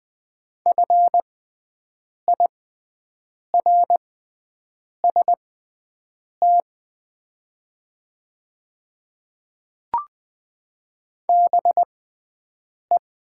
Morse code, heard as a keyed tone, about 20 wpm, spaced as for 7 wpm, 700 Hz. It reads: FIRST BE
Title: 6x Character Spacing - Top 100 Words - 20wpm